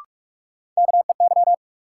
Morse code, heard as a keyed tone, about 45 wpm, 700 Hz. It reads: KEY
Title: Sentences from Top 600 Words - New Word Drill - 45wpm